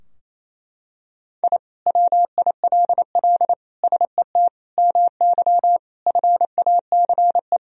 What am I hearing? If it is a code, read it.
Morse code, heard as a keyed tone, about 27 wpm, 700 Hz. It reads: I WILL SET MY FACE
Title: Sentences from Top 1000 Words - Review Part 2 - 28wpm